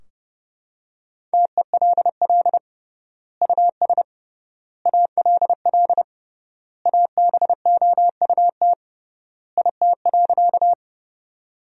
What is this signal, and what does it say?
Morse code, heard as a keyed tone, about 32 wpm, 700 Hz. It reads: TELL US ALL ABOUT IT.
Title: 3x Word-Spacing - Sentences from Top 200 Words - 30wpm